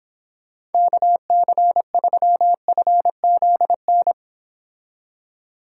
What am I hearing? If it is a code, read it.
Morse code, heard as a keyed tone, about 29 wpm, 700 Hz. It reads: KC3FZN